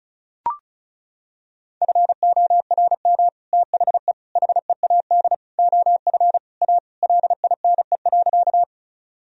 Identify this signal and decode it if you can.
Morse code, heard as a keyed tone, about 34 wpm, 700 Hz. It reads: FORM THE HEAD OF A LINE.
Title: Sentences from Top 1000 Words - Review Part 1 - 35wpm